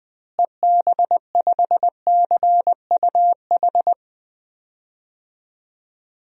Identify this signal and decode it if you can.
Morse code, heard as a keyed tone, about 21 wpm, 700 Hz. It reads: EB5CUH
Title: DXCC-100 Call Signs - 6 Characters - Repeated 3x - 20wpm